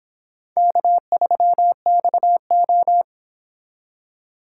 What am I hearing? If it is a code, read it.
Morse code, heard as a keyed tone, about 28 wpm, 700 Hz. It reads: K3XO